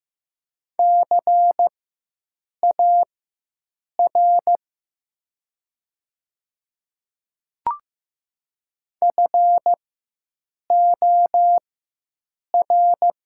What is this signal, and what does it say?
Morse code, heard as a keyed tone, about 15 wpm, spaced as for 7 wpm, 700 Hz. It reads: CAR FOR
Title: 4x Character Spacing - Top 300 Words - 15wpm